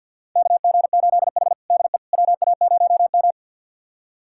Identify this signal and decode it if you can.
Morse code, heard as a keyed tone, about 51 wpm, 700 Hz. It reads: KG8H DE WA0M
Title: USA Callsign DE Callsign - 4 Characters - 50wpm